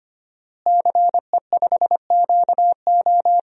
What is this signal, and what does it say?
Morse code, heard as a keyed tone, about 25 wpm, 700 Hz. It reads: CE5QO
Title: DXCC-100 Call Signs - 5 Characters - Repeated 2x - 25wpm